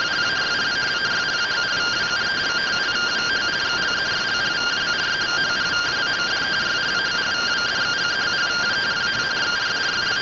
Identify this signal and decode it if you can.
Two-tone teleprinter signal, about 170 Hz shift.
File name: findings / AXIAL fr1 3mm.gen-28 rtty rx